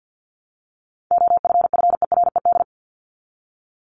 Morse code, heard as a keyed tone, about 50 wpm, 700 Hz. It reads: OFFERED
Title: ED Suffix - Top 100 Words with Base Words 50wpm